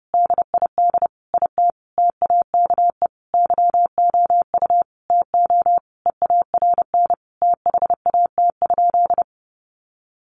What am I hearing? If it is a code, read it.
Morse code, heard as a keyed tone, about 28 wpm, 700 Hz. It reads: DID IT TAKE YOU TO EARN THAT?
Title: Sentences from Top 1500 Words 30wpm